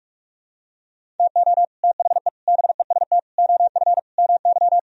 Morse code, heard as a keyed tone, about 44 wpm, 700 Hz. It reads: TO THE BEST OF MY
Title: Sentences from Top 1000 Words - Review Part 3 - 45wpm